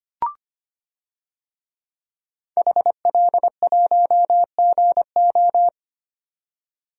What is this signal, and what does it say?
Morse code, heard as a keyed tone, about 25 wpm, 700 Hz. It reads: HL1GO
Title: DXCC-100 Call Signs - 5 Characters - 25wpm